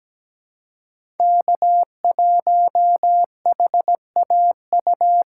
Morse code, heard as a keyed tone, about 17 wpm, 700 Hz. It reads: K1HAU
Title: USA Call Signs - 5 Characters - 4x Character Spacing - 17wpm